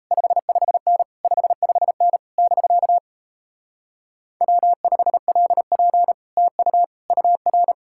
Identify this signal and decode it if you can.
Morse code, heard as a keyed tone, about 38 wpm, 700 Hz. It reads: 55N 55N <BK> W5LP TU UR